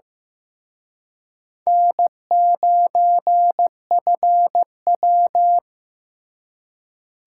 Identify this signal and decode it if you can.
Morse code, heard as a keyed tone, about 18 wpm, 700 Hz. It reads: N9FW
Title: Speed Racing - USA Call Signs 15wpm